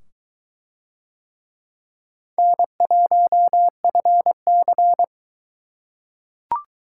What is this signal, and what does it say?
Morse code, heard as a keyed tone, about 23 wpm, 700 Hz. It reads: N1FC